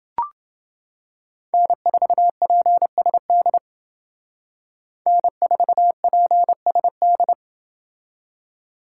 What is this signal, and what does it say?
Morse code, heard as a keyed tone, about 28 wpm, 700 Hz. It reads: N4PSD N4PSD